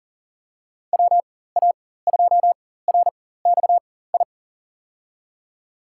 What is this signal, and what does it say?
Morse code, heard as a keyed tone, about 40 wpm, spaced as for 19 wpm, 700 Hz. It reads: WA2RXI
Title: USA Call Signs - 6 Characters - 4x Character Spacing - 40wpm